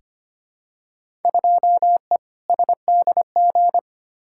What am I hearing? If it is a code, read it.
Morse code, heard as a keyed tone, about 25 wpm, 700 Hz. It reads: 2E SDG
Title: Field Day Exchanges - 25wpm